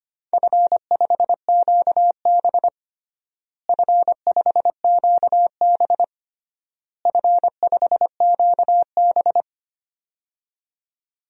Morse code, heard as a keyed tone, about 25 wpm, 700 Hz. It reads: F5QB F5QB F5QB
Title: DXCC-100 Call Signs - 4 Characters - Repeated 3x - 25wpm